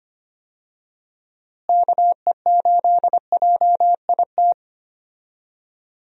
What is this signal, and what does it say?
Morse code, heard as a keyed tone, about 25 wpm, 700 Hz. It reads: KE8JIT
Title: USA Call Signs - 6 Characters - Repeated 2x - 25wpm